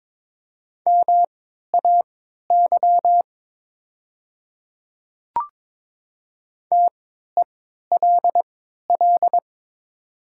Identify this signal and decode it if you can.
Morse code, heard as a keyed tone, about 22 wpm, spaced as for 13 wpm, 700 Hz. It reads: MAY TELL
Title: 3x Character Spacing - Top 200 Words - 22wpm